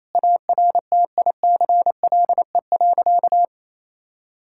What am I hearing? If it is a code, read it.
Morse code, heard as a keyed tone, about 28 wpm, 700 Hz. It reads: ARTICLE.
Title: Sentences from Top 1000 Words - Review Part 5 - 28wpm